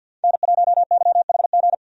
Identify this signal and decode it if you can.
Morse code, heard as a keyed tone, about 49 wpm, 700 Hz. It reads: N1YHG
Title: USA Call Signs - 5 Characters - 50wpm